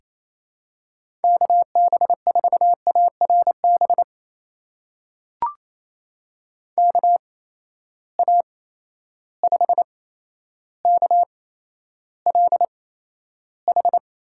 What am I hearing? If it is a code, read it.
Morse code, heard as a keyed tone, about 28 wpm, spaced as for 12 wpm, 700 Hz. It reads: KB4ARB KA5KLH
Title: USA Call Signs - 6 Characters - 8x Character Spacing - 28wpm